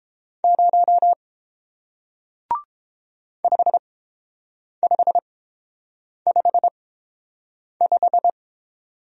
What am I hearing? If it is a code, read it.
Morse code, heard as a keyed tone, about 30 wpm, spaced as for 7 wpm, 700 Hz. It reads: 0 5555